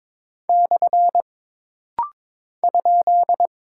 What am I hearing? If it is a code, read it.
Morse code, heard as a keyed tone, about 22 wpm, 700 Hz. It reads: / ?
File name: practice / 22wpm / Single Punctuation-Prosign - Rapid-Fire - 22wpm